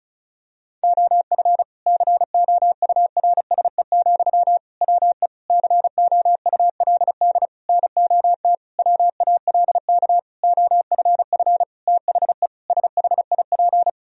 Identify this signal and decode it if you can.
Morse code, heard as a keyed tone, about 34 wpm, 700 Hz. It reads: OF COURSE, WE COULD NOT WALK OFF THE SHIP
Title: Sentences from Top 1000 Words - Review Part 2 - 35wpm